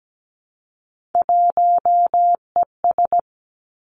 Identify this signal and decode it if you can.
Morse code, heard as a keyed tone, about 17 wpm, 700 Hz. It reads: 1ES
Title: Three Challenging Letter-Number Combinations 17wpm